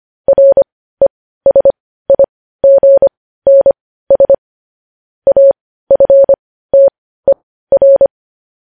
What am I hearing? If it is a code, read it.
Morse code, heard as a keyed tone, about 25 wpm, spaced as for 15 wpm, 550 Hz. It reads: RESIGNS AFTER
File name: podcast / News headlines-1x-25x15